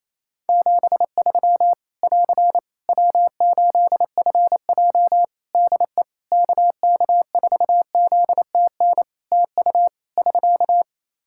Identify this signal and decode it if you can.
Morse code, heard as a keyed tone, about 28 wpm, 700 Hz. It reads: 73 <AR> W8FJ DE KK4ZTN TU <SK>
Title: Practice QSO - Exchange 4 - Ending - 28wpm